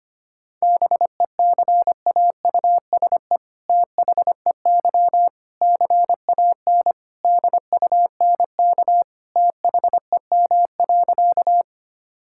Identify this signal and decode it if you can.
Morse code, heard as a keyed tone, about 25 wpm, 700 Hz. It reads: BECAUSE THEY CAN DUNK THEM.
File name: practice / 25wpm / All-Age Appropriate Jokes - 25wpm